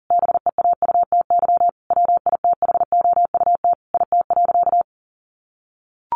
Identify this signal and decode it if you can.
Morse code, heard as a keyed tone, about 37 wpm, 700 Hz. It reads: BEAUTY WITHOUT IT.
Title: Sentences from Top 600 Words 40wpm